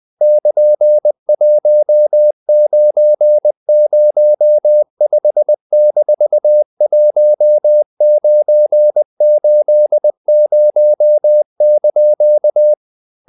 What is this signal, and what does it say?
Morse code, heard as a keyed tone, about 20 wpm, 600 Hz. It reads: <KN>1905-1980)
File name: CW / Practice / EnglishQuotations0320